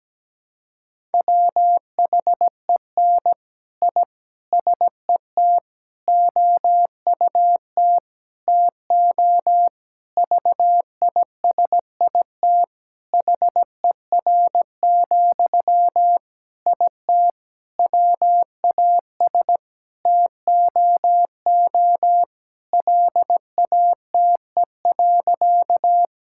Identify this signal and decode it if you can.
Morse code, heard as a keyed tone, about 17 wpm, 700 Hz. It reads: WHEN I SET OUT TO VISIT HER, IT WAS TOO LATE.
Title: Sentences from Top 1000 Words - Review Part 6 - 17wpm